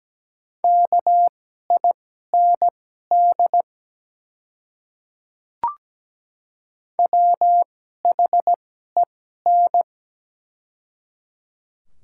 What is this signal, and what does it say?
Morse code, heard as a keyed tone, about 17 wpm, spaced as for 12 wpm, 700 Hz. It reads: KIND WHEN
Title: 2x Character Spacing - Top 200 Words - 17wpm